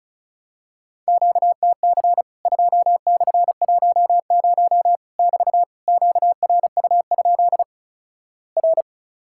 Morse code, heard as a keyed tone, about 36 wpm, 700 Hz. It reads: QTC 2/10 <BT> QRU? R
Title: Worked All Europe DX Contest - QTC Exchange - 35wpm